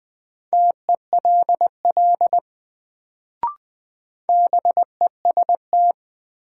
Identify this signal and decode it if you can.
Morse code, heard as a keyed tone, about 20 wpm, 700 Hz. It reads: TELL BEST
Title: Sets of 2 Words - New Word Drill - 20wpm